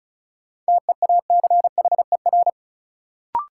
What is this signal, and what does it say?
Morse code, heard as a keyed tone, about 36 wpm, 700 Hz. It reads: TEACHER
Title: Sentences from Top 900 Words - New Word Drill - 35wpm